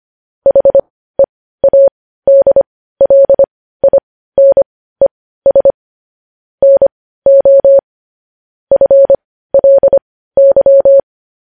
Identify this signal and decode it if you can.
Morse code, heard as a keyed tone, about 25 wpm, spaced as for 15 wpm, 550 Hz. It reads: HEADLINES NO FLY
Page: News Headlines in Morse Code at 15 WPM